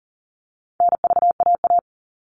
Morse code, heard as a keyed tone, about 40 wpm, 700 Hz. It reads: NVAA
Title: All Cut Numbers - 4-Digit 40wpm